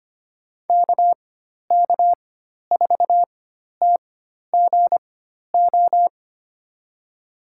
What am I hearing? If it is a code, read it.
Morse code, heard as a keyed tone, about 25 wpm, spaced as for 12 wpm, 700 Hz. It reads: KK4TGO
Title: USA Call Signs - 4x Character Spacing - 25wpm